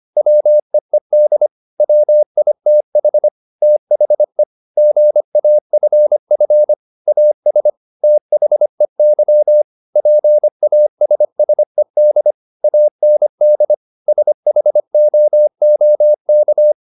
Morse code, heard as a keyed tone, about 25 wpm, 600 Hz. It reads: WEED WITH THE GAFF AS THEY PASSED AND SHOOK